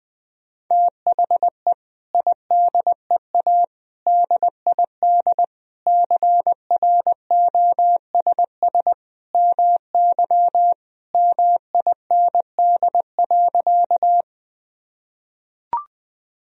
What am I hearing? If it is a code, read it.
Morse code, heard as a keyed tone, about 21 wpm, 700 Hz. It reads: THE IDEA DID CROSS MY MIND.
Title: Sentences from Top 1000 Words - Review Part 1 - 20wpm